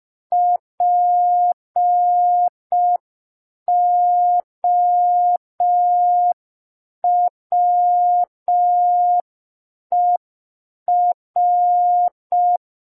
Morse code, heard as a keyed tone, about 5 wpm, 710 Hz. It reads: POWER
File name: g3zpd / Morse code P4